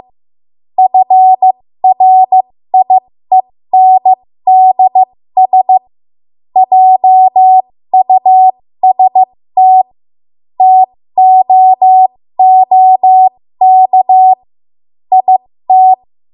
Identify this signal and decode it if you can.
Morse code, heard as a keyed tone, about 15 wpm, spaced as for 13 wpm, 750 Hz. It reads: FRIENDS JUST TOOK IT
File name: Morse / 13WPM